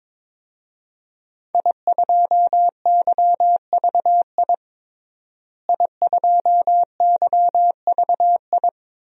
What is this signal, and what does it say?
Morse code, heard as a keyed tone, about 21 wpm, 700 Hz. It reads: I2YVI I2YVI
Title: DXCC-100 Call Signs - 5 Characters - Repeated 3x - 22wpm